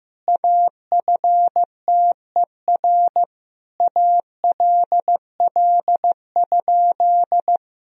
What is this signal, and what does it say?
Morse code, heard as a keyed tone, about 15 wpm, 700 Hz. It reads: AFTER ALL?
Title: Sentences from Top 1000 Words - Review Part 2 - 15wpm